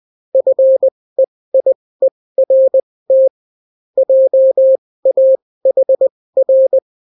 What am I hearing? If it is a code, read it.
Morse code, heard as a keyed tone, about 20 wpm, spaced as for 16 wpm, 530 Hz. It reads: FEIERT JAHR